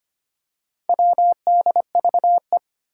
Morse code, heard as a keyed tone, about 25 wpm, 700 Hz. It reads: WDVE
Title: Four Challenging Letter-Number Combinations 25wpm